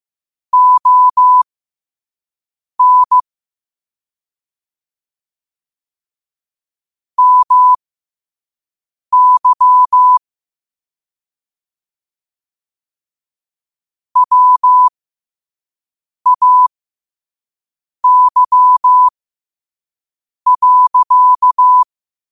Morse code, heard as a keyed tone, about 15 wpm, spaced as for 5 wpm, 1000 Hz. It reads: ON MY WAY.
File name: qotdfiles / 5wpm